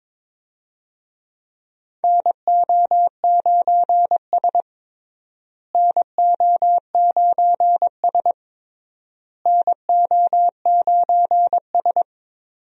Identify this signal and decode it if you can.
Morse code, heard as a keyed tone, about 22 wpm, 700 Hz. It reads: NO9S NO9S NO9S
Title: USA Call Signs - 4 Characters - Repeated 3x - 22wpm